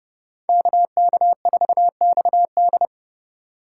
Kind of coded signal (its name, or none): Morse code